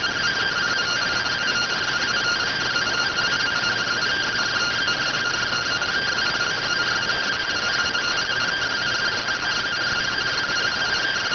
Two-tone teleprinter signal, about 170 Hz shift.